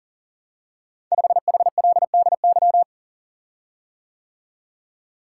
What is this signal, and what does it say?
Morse code, heard as a keyed tone, about 42 wpm, 690 Hz. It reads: 5HLDY
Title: Code Groups - 5 Letter:Numbers - 40wpm